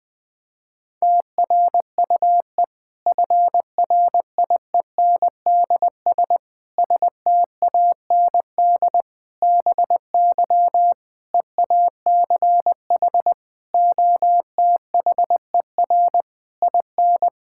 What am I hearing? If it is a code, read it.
Morse code, heard as a keyed tone, about 21 wpm, 700 Hz. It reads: TRUE FRIENDS STAND BY EACH OTHER IN
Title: Difficult Sentences with Hints - 20wpm